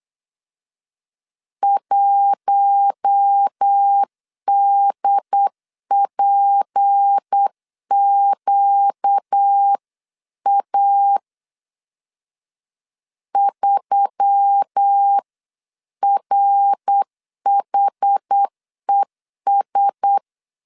Morse code, heard as a keyed tone, about 8 wpm, 800 Hz. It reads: 1DPQA 3RHES
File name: 8wpm